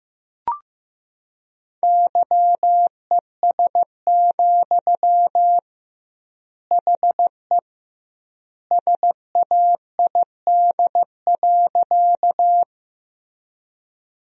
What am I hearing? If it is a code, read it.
Morse code, heard as a keyed tone, about 15 wpm, 700 Hz. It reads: YES, HE SAID.
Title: 2x Word-Spacing - Sentences from Top 500 Words - 15wpm